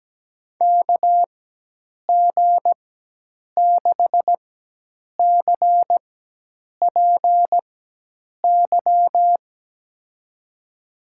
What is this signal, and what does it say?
Morse code, heard as a keyed tone, about 17 wpm, spaced as for 8 wpm, 700 Hz. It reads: KG6CPY
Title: USA Call Signs - 6 Characters - 4x Character Spacing - 17wpm